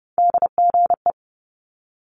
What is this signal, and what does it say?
Morse code, heard as a keyed tone, about 30 wpm, 700 Hz. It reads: DGE
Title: Multi-letter Phonemes Quick Fire 30wpm